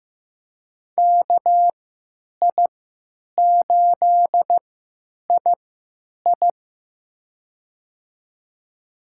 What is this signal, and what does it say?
Morse code, heard as a keyed tone, about 15 wpm, spaced as for 9 wpm, 700 Hz. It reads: KI8II
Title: USA Call Signs - 3x Character Spacing - 15wpm